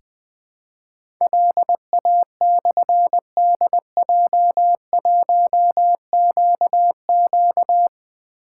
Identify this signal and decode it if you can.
Morse code, heard as a keyed tone, about 20 wpm, 700 Hz. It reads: LA/DJ1QQ
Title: Difficult Call Signs 20wpm